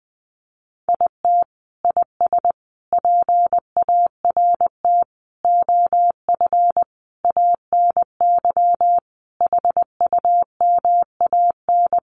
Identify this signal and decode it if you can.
Morse code, heard as a keyed tone, about 19 wpm, 700 Hz. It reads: IT IS PART OF ANY HUMAN
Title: Sentences from Top 1100 Words 20wpm